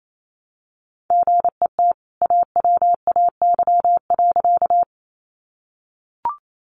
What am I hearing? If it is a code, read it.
Morse code, heard as a keyed tone, about 28 wpm, 700 Hz. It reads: GET AWAY.
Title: Sentences from Top 600 Words 28wpm